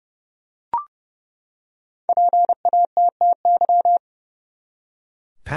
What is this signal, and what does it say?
Morse code, heard as a keyed tone, about 31 wpm, 700 Hz. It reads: PATTY
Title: USA Common Names - Max 5 Letters - 30wpm